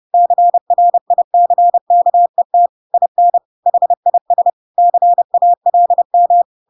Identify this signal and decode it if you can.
Morse code, heard as a keyed tone, about 30 wpm, 700 Hz. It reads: CRICKET IN HIS CALM
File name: Pinocchio / Pinocchio ch4 30 30 ik1hge